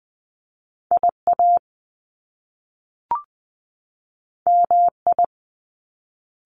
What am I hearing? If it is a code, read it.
Morse code, heard as a keyed tone, about 20 wpm, 700 Hz. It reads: IA MI
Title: Top Two Letter Combinations in Words 20wpm